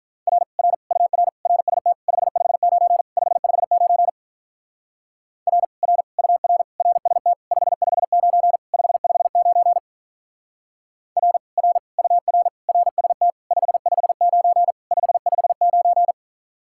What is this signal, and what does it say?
Morse code, heard as a keyed tone, about 42 wpm, 700 Hz. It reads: R R UR RST 559 559 R R UR RST 559 559 R R UR RST 559 559
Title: Speed-Racing Common QSO Phrases - 35wpm